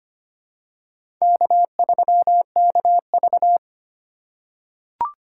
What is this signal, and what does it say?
Morse code, heard as a keyed tone, about 25 wpm, 700 Hz. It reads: K3KV
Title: USA Call Signs - 2x Character Spacing - 25wpm